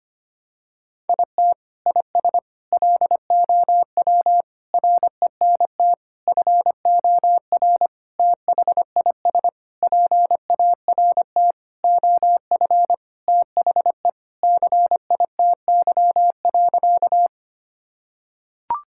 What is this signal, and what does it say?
Morse code, heard as a keyed tone, about 25 wpm, 700 Hz. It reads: IT IS LOW RENT FOR THIS PART OF THE CITY.
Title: Sentences from Top 1000 Words - Review Part 5 - 25wpm